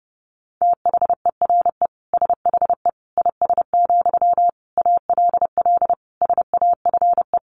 Morse code, heard as a keyed tone, about 30 wpm, 700 Hz. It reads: THERE SHE IS, ALL SAFE
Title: Sentences from Top 900 Words 30wpm